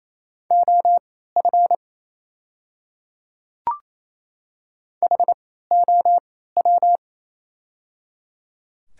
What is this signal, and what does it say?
Morse code, heard as a keyed tone, about 28 wpm, spaced as for 16 wpm, 690 Hz. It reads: OF HOW
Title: 3x Character Spacing - Top 100 Words - 28wpm